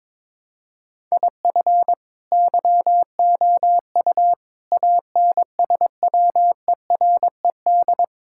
Morse code, heard as a keyed tone, about 22 wpm, 700 Hz. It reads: IF YOU ANSWERED
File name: Sets of 9 Words - 22wpm